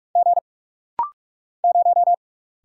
Morse code, heard as a keyed tone, about 43 wpm, 700 Hz. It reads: G 0
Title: Single Letter-Number Alt - Rapid-Fire 45wpm